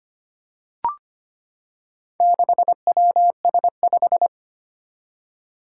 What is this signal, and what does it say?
Morse code, heard as a keyed tone, about 25 wpm, 700 Hz. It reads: 6WS5
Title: Four Challenging Letter-Number Combinations 25wpm